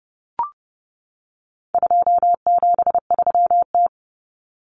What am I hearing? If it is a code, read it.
Morse code, heard as a keyed tone, about 30 wpm, 700 Hz. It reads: 273T
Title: Common Cut Numbers - 4-Digit 30wpm